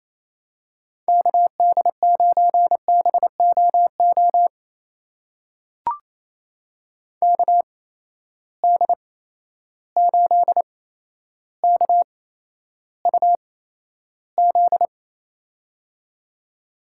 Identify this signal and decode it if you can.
Morse code, heard as a keyed tone, about 28 wpm, spaced as for 12 wpm, 700 Hz. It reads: KD9BOO KD8KUZ
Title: USA Call Signs - 6 Characters - 8x Character Spacing - 28wpm